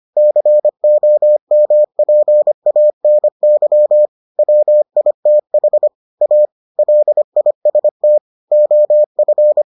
Morse code, heard as a keyed tone, about 25 wpm, 600 Hz. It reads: COMPANY WITH A LIST OF